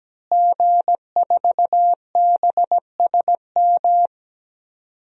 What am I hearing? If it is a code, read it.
Morse code, heard as a keyed tone, about 17 wpm, 700 Hz. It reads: G4BSM
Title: DXCC-100 Call Signs - 5 Characters - Repeated 3x - 17wpm